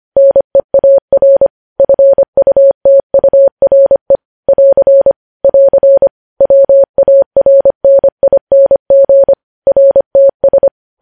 Morse code, heard as a keyed tone, about 25 wpm, 550 Hz. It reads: NEAR FUTURE <AR> <AR> WARNING RTS
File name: podcast / News headlines-1x-25x25